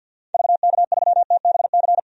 Morse code, heard as a keyed tone, about 47 wpm, 700 Hz. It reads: VK3TBC
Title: DXCC-100 Call Signs - 6 Characters - Repeated 3x - 50wpm